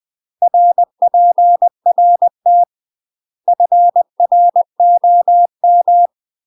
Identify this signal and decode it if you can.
Morse code, heard as a keyed tone, about 20 wpm, 700 Hz. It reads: RPRT FROM
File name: UK MASTER QSOs TRAINING / 20WPM / UK MASTER QSOs 0067